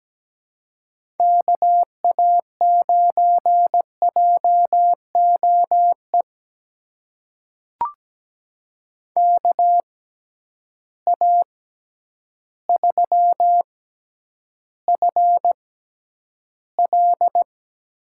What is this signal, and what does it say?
Morse code, heard as a keyed tone, about 17 wpm, spaced as for 9 wpm, 700 Hz. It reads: KA9JOE KA3FL
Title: USA Call Signs - 6 Characters - 6x Character Spacing - 17wpm